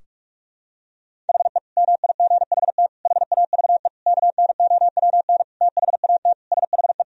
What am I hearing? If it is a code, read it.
Morse code, heard as a keyed tone, about 46 wpm, 700 Hz. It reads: HE MIGHT HAVE KNOWN THAT SHE
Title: Sentences from Top 1000 Words - Review Part 3 - 45wpm